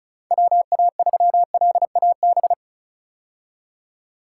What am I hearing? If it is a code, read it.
Morse code, heard as a keyed tone, about 36 wpm, 700 Hz. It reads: WA3LAB
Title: K1USN SST QSO Practice - 35wpm